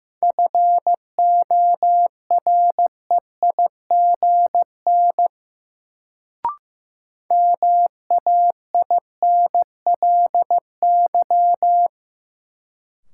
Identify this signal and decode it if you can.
Morse code, heard as a keyed tone, about 15 wpm, 700 Hz. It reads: FOREIGN MAINLY
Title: Sentences from Top 1400 Words - New Word Drill Alt - 15wpm